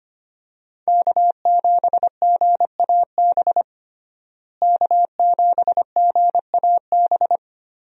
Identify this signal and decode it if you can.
Morse code, heard as a keyed tone, about 25 wpm, 700 Hz. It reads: K7GAB K7GAB
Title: USA Call Signs - 5 Characters - Repeated 2x - 25wpm